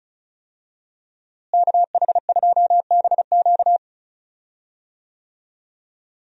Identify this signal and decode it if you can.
Morse code, heard as a keyed tone, about 36 wpm, 700 Hz. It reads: KH2BQ
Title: USA Call Signs - 5 Characters - Repeated 2x - 35wpm